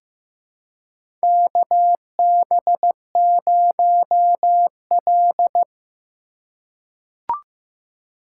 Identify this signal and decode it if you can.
Morse code, heard as a keyed tone, about 15 wpm, 700 Hz. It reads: KB0L
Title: USA Call Signs - 4 Characters - 3x Character Spacing - 15wpm